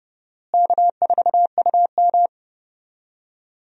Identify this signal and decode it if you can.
Morse code, heard as a keyed tone, about 30 wpm, 700 Hz. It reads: K4UM
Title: USA Call Signs - 4 Characters - Repeated 3x - 30wpm